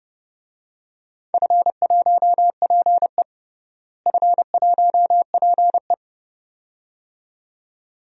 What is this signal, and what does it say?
Morse code, heard as a keyed tone, about 30 wpm, 700 Hz. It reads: F1PE F1PE
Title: DXCC-100 Call Signs - 4 Characters - Repeated 2x - 30wpm